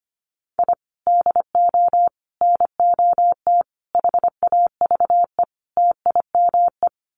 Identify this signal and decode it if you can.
Morse code, heard as a keyed tone, about 25 wpm, 700 Hz. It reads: I DO NOT HAVE TIME
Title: Sets of 5 Words Encore 25wpm